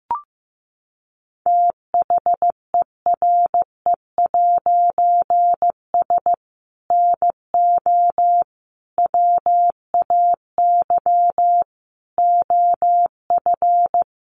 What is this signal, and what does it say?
Morse code, heard as a keyed tone, about 15 wpm, 700 Hz. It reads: THERE'S NO WAY OF
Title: A Sound Of Thunder 15wpm